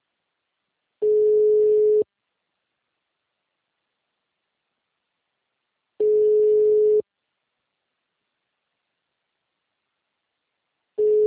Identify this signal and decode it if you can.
Ringback tone (Europe, 425 Hz).